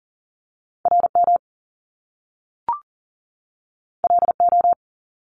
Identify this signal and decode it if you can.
Morse code, heard as a keyed tone, about 38 wpm, 700 Hz. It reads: RM LO
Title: Top Two Letter Combinations in Words 40wpm